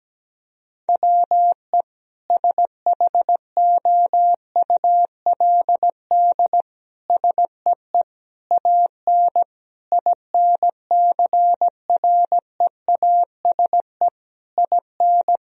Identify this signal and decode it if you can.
Morse code, heard as a keyed tone, about 17 wpm, 700 Hz. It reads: WE SHOULD SEE AN INCREASE IN
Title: Sentences from Top 1000 Words - Review Part 4 - 17wpm